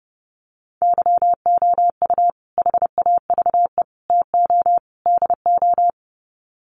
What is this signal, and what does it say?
Morse code, heard as a keyed tone, about 30 wpm, 700 Hz. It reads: YOU HAVE TO DO
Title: Sets of 4 Words 30wpm